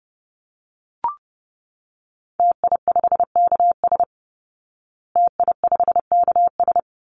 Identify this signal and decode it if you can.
Morse code, heard as a keyed tone, about 30 wpm, 700 Hz. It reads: TI5KS TI5KS
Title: Call Signs Repeated 2x - 30wpm